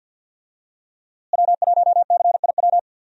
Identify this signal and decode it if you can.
Morse code, heard as a keyed tone, about 50 wpm, 700 Hz. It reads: W1CIW
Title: USA Call Signs - 5 Characters - Repeated 2x - 50wpm